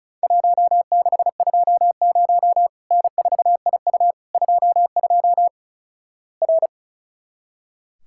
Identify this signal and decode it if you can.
Morse code, heard as a keyed tone, about 36 wpm, 700 Hz. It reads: 1620 N4IU 22 R
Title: Worked All Europe DX Contest - QTC Exchange - 35wpm